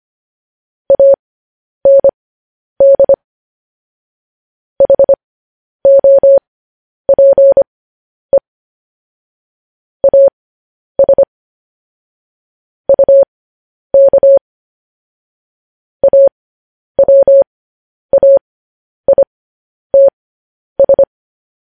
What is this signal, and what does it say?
Morse code, heard as a keyed tone, about 25 wpm, spaced as for 10 wpm, 550 Hz. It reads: AND HOPE AS UK AWAITS
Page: news headlines in morse code at 10 wpm